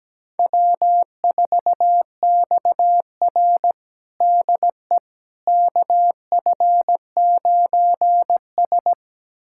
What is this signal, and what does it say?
Morse code, heard as a keyed tone, about 17 wpm, 700 Hz. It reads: W4XR DE KF9S
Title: USA Callsign DE Callsign - 4 Characters - 17wpm